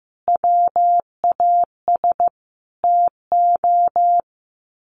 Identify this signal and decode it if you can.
Morse code, heard as a keyed tone, about 15 wpm, 700 Hz. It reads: WAS TO
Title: Sets of 5 Words 15wpm